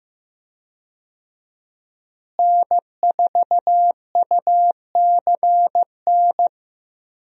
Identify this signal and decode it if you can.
Morse code, heard as a keyed tone, about 15 wpm, 700 Hz. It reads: N4UCN
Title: USA Call Signs - 5 Characters - Repeated 3x - 15wpm